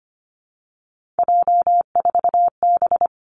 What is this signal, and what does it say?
Morse code, heard as a keyed tone, about 25 wpm, 700 Hz. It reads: J4B
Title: Three Challenging Letter-Number Combinations 25wpm